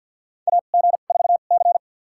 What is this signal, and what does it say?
Morse code, heard as a keyed tone, about 48 wpm, spaced as for 35 wpm, 700 Hz. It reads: AG4C
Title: USA Call Signs - 4 Characters - 2x Character Spacing - 50wpm